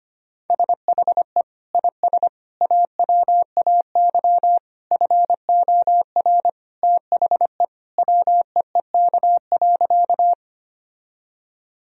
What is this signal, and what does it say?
Morse code, heard as a keyed tone, about 25 wpm, 700 Hz. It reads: SHE IS AWAY FOR THE WEEK.
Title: Sentences from Top 1000 Words - Review Part 2 - 25wpm